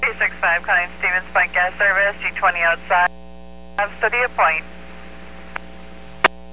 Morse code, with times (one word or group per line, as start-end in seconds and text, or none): none